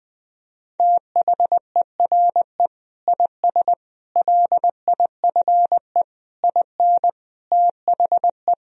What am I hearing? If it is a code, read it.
Morse code, heard as a keyed tone, about 20 wpm, 700 Hz. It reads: THERE IS LIFE IN THE
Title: Sentences from Top 1000 Words - Review Part 1 - 20wpm